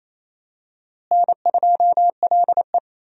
Morse code, heard as a keyed tone, about 28 wpm, 700 Hz. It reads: N2LE
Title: USA Call Signs - 4 Characters - Repeated 3x - 28wpm